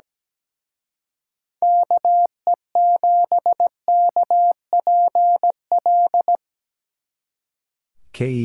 Morse code, heard as a keyed tone, about 19 wpm, 700 Hz. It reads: KE7KPL
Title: Speed Racing - USA Call Signs 17wpm